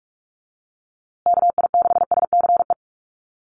Morse code, heard as a keyed tone, about 44 wpm, 700 Hz. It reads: KI6SCE
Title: USA Call Signs Repeated 2x 45wpm